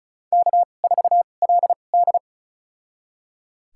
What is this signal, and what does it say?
Morse code, heard as a keyed tone, about 36 wpm, spaced as for 25 wpm, 700 Hz. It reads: K4LD
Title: USA Call Signs - 4 Characters - 2x Character Spacing - 35wpm